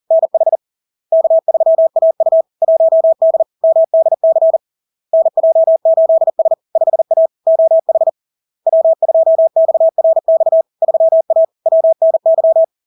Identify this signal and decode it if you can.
Morse code, heard as a keyed tone, about 40 wpm, 650 Hz. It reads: NH K3AU 1D MDC NJ8S 5A OH W2XRX 3A WNY
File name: LARK-FD-2022-Log-in-CW-40wpm-650-Hz